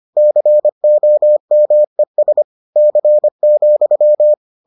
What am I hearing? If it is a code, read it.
Morse code, heard as a keyed tone, about 25 wpm, 600 Hz. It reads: COMES C,